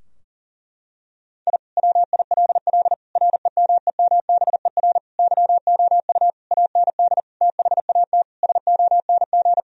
Morse code, heard as a keyed tone, about 40 wpm, 700 Hz. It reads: I WILL REMEMBER YOU AND THAT SONG
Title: Sentences from Top 1000 Words - Review Part 2 - 40wpm